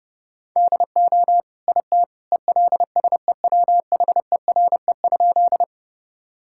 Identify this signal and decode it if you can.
Morse code, heard as a keyed tone, about 32 wpm, 700 Hz. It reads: DO IT ELSEWHERE?
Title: Sentences from Top 1600 Words 30wpm